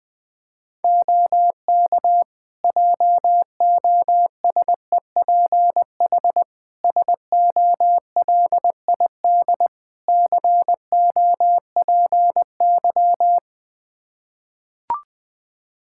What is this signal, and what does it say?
Morse code, heard as a keyed tone, about 20 wpm, 700 Hz. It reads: OK JOSEPH SOLID COPY
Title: Practice QSO - Exchange 3 - More Info - 20wpm